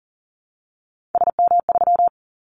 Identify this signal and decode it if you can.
Morse code, heard as a keyed tone, about 40 wpm, 700 Hz. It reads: SM3
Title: Callsign Prefixes for Top 25 DXCC - Repeated 3x - 40wpm